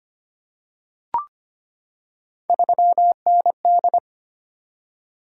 Morse code, heard as a keyed tone, about 25 wpm, 700 Hz. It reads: 3ND
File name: Three Letter-Number Combinations - 25wpm